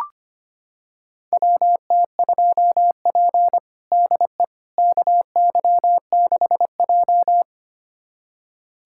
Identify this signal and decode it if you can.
Morse code, heard as a keyed tone, about 25 wpm, 700 Hz. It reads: WT2P DE KY6J
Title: USA Callsign DE Callsign - 4 Characters - 25wpm